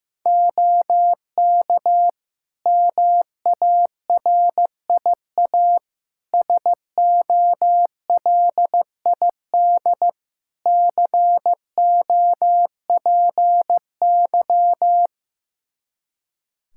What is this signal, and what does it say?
Morse code, heard as a keyed tone, about 15 wpm, 700 Hz. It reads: OK MARIA SOLID COPY
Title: Practice QSO - Exchange 3 - More Info - 15wpm